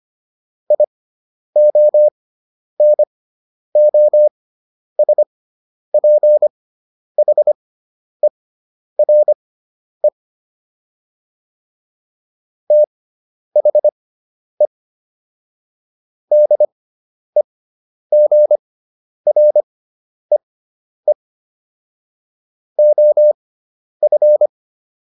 Morse code, heard as a keyed tone, about 25 wpm, spaced as for 10 wpm, 600 Hz. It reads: IONOSPHERE THE DEGREE OF